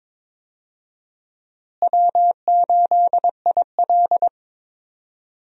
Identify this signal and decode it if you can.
Morse code, heard as a keyed tone, about 22 wpm, 700 Hz. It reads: W8IL
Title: USA Call Signs - 4 Characters - Repeated 2x - 22wpm